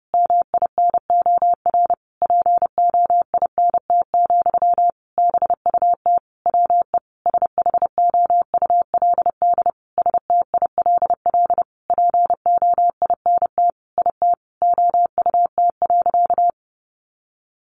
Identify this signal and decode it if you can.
Morse code, heard as a keyed tone, about 30 wpm, 700 Hz. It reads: MINOR POINT, BUT WE SHOULD STILL POINT IT OUT.
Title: Sentences from Top 1500 Words 30wpm